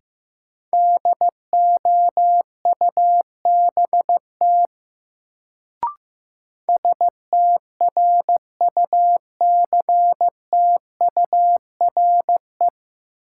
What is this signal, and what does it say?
Morse code, heard as a keyed tone, about 16 wpm, 700 Hz. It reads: DOUBT STRUCTURE
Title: Sentences from Top 1500 Words - New Word Drill Alt - 15wpm